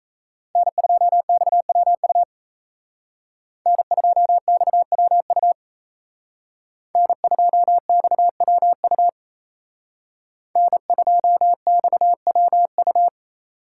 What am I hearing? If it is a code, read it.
Morse code, heard as a keyed tone, about 36 wpm, 700 Hz. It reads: N2XWU N2XWU N2XWU N2XWU